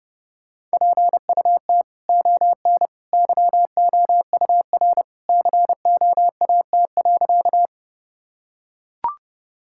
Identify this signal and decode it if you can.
Morse code, heard as a keyed tone, about 30 wpm, 700 Hz. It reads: PUT ON YOUR COAT.
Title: Sentences from Top 1600 Words 30wpm